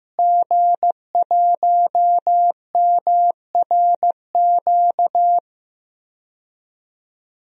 Morse code, heard as a keyed tone, about 15 wpm, 700 Hz. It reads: G1MRQ
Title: DXCC-100 Call Signs - 5 Characters - Repeated 2x - 15wpm